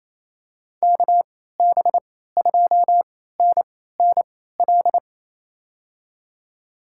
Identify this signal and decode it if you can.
Morse code, heard as a keyed tone, about 27 wpm, spaced as for 16 wpm, 700 Hz. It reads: KB2NNL
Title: USA Call Signs - 6 Characters - 3x Character Spacing - 28wpm